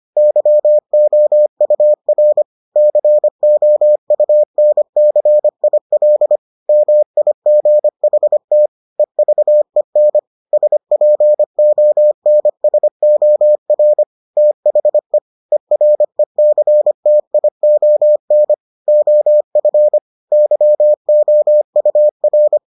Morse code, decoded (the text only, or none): YOUR COUNCIL MIGHT EVEN SPONSOR THE ERECTION OF YOUR